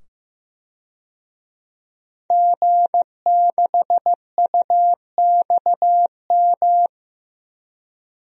Morse code, heard as a keyed tone, about 15 wpm, 700 Hz. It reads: G6UXM
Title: DXCC-100 Call Signs - 5 Characters - 15wpm